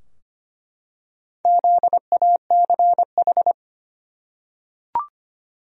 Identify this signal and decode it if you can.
Morse code, heard as a keyed tone, about 25 wpm, 700 Hz. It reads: ZACH
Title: USA Common Names - 25wpm